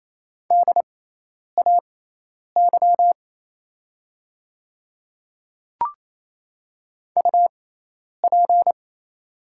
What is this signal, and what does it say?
Morse code, heard as a keyed tone, about 28 wpm, spaced as for 10 wpm, 700 Hz. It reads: DAY UP
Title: 6x Character Spacing - Top 100 Words - 28wpm